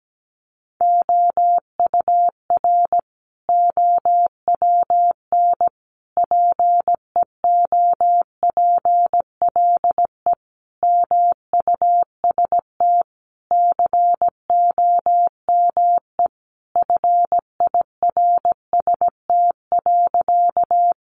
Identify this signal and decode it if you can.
Morse code, heard as a keyed tone, about 17 wpm, 700 Hz. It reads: OUR OWN PEOPLE MUST COME FIRST.
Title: Sentences from Top 300 Words 17wpm